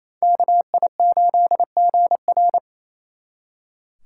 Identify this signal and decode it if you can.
Morse code, heard as a keyed tone, about 35 wpm, 700 Hz. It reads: KI8GR